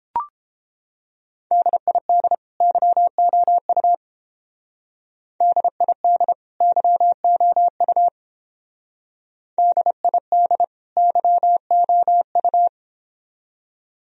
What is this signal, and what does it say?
Morse code, heard as a keyed tone, about 29 wpm, 700 Hz. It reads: DID YOU DID YOU DID YOU